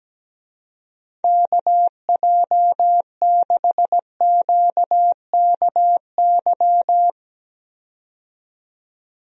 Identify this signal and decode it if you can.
Morse code, heard as a keyed tone, about 17 wpm, 700 Hz. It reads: KJ6QKY
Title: USA Call Signs - 6 Characters - 17wpm